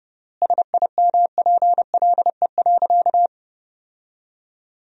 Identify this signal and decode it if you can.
Morse code, heard as a keyed tone, about 30 wpm, 700 Hz. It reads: SIMPLE.
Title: 3x Word-Spacing - Sentences from Top 500 Words - 30wpm